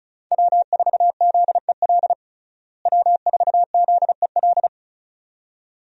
Morse code, heard as a keyed tone, about 36 wpm, 700 Hz. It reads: W4ZEL W4ZEL
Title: USA Call Signs - 5 Characters - Repeated 2x - 35wpm